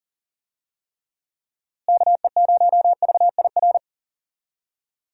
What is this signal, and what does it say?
Morse code, heard as a keyed tone, about 40 wpm, 700 Hz. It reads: KE0VIR
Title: USA Call Signs - 6 Characters - Repeated 3x - 40wpm